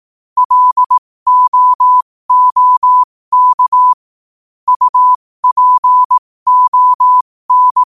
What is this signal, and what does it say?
Morse code, heard as a keyed tone, about 18 wpm, spaced as for 15 wpm, 1000 Hz. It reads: LOOK UPON
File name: qotdfiles / 15wpm